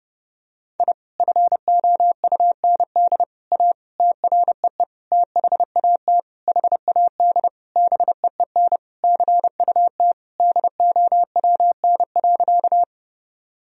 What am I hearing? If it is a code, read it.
Morse code, heard as a keyed tone, about 32 wpm, 700 Hz. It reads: I FOUND A TREE THAT HAD BEEN CUT DOWN.
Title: Sentences from Top 1000 Words - Review Part 1 - 30wpm